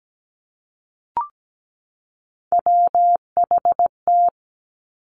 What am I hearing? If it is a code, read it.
Morse code, heard as a keyed tone, about 17 wpm, 700 Hz. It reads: WHT
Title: Three Challenging Letter-Number Combinations 17wpm